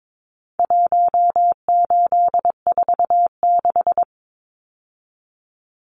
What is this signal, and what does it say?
Morse code, heard as a keyed tone, about 22 wpm, 700 Hz. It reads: 1846
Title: Numbers - Contest Style - 4-Digit 22wpm